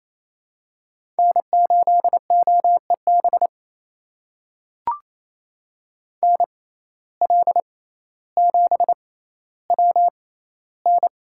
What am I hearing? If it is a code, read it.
Morse code, heard as a keyed tone, about 28 wpm, spaced as for 14 wpm, 700 Hz. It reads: N8OEB NL7WN
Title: USA Call Signs - 5 Characters - 6x Character Spacing - 28wpm